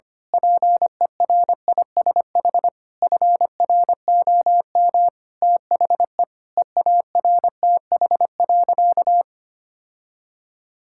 Morse code, heard as a keyed tone, about 25 wpm, 700 Hz. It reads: PERISH FROM THE EARTH.
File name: practice / 25wpm / Sentences with 9-Word Phrases - 25wpm